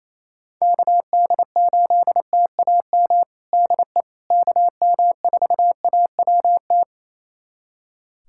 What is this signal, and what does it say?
Morse code, heard as a keyed tone, about 27 wpm, 700 Hz. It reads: KD8TAM DE KM4AWT
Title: USA Callsign DE Callsign - 6 Characters - 28wpm